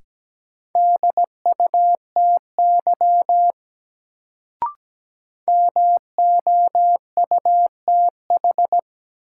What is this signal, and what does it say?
Morse code, heard as a keyed tone, about 17 wpm, 700 Hz. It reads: DUTY MOUTH
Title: Sentences from Top 1500 Words - New Word Drill - 17wpm